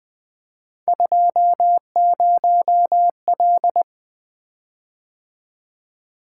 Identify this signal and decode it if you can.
Morse code, heard as a keyed tone, about 20 wpm, 700 Hz. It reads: 20L
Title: Three Letter-Number Combinations - 20wpm